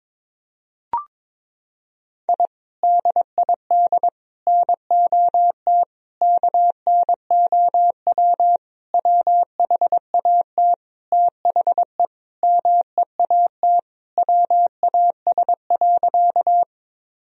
Morse code, heard as a keyed tone, about 22 wpm, 700 Hz. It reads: I DID NOT KNOW WHAT THE MEAT WAS.
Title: Sentences from Top 1000 Words - Review Part 3 - 22wpm